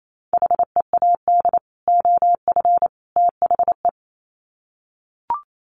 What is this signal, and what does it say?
Morse code, heard as a keyed tone, about 28 wpm, 700 Hz. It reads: HEAD OF THE
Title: Sets of 3 Words 28wpm